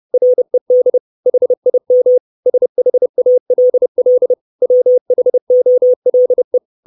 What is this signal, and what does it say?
Morse code, heard as a keyed tone, about 30 wpm, 500 Hz. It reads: RED HIM SHALL WHOLE
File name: 632 Words 30WPM